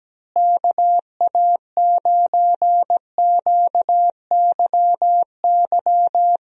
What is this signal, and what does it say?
Morse code, heard as a keyed tone, about 17 wpm, 700 Hz. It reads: KA9QYY
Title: USA Call Signs - 6 Characters - Repeated 2x - 17wpm